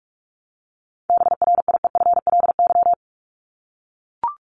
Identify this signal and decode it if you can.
Morse code, heard as a keyed tone, about 44 wpm, 700 Hz. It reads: BRIEFLY
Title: LY Suffix - Top 100 Words with Base Words 45wpm